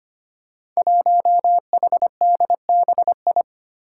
Morse code, heard as a keyed tone, about 25 wpm, 700 Hz. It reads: 1HDBI
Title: Five Challenging Letter-Number Combinations 25wpm